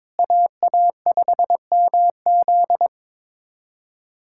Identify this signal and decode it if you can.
Morse code, heard as a keyed tone, about 22 wpm, 700 Hz. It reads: AA5MZ
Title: USA Callsign DE Callsign - 5 Characters - 22wpm